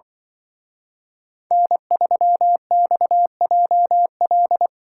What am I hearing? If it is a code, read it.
Morse code, heard as a keyed tone, about 23 wpm, 700 Hz. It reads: N3XJL
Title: Speed Racing - USA 1x3 Call Signs 20wpm